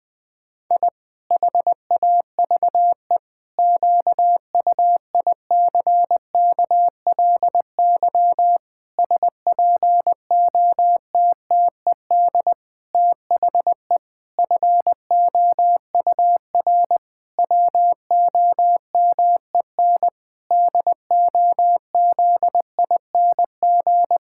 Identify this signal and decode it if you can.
Morse code, heard as a keyed tone, about 20 wpm, 700 Hz. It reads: I HAVE QUICKLY SPOTTED THE FOUR WOMEN DOZING
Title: Pangram Challenge v2 - 20wpm